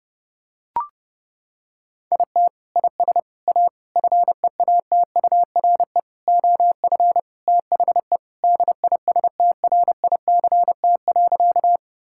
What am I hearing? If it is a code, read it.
Morse code, heard as a keyed tone, about 32 wpm, 700 Hz. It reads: IT IS A FEATURE OF THE DISTRICT.
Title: Sentences from Top 1600 Words 30wpm